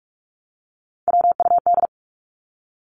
Morse code, heard as a keyed tone, about 44 wpm, 700 Hz. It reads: WUD
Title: Abbreviations Repeated 3x - 45wpm